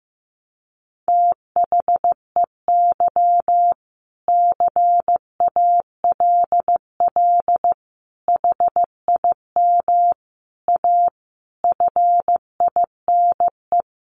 Morse code, heard as a keyed tone, about 15 wpm, 700 Hz. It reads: THEY CALL HIM A FINE
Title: Sentences from Top 500 Words 15wpm